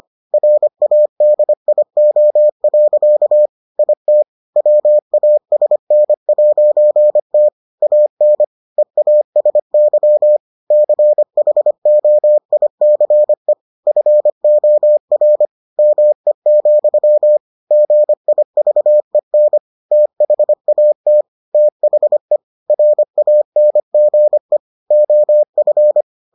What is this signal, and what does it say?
Morse code, heard as a keyed tone, about 25 wpm, 600 Hz. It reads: RADIO. IT WASN'T AN EASY CHOICE FOR ME, GIVEN THAT THE RANGE OF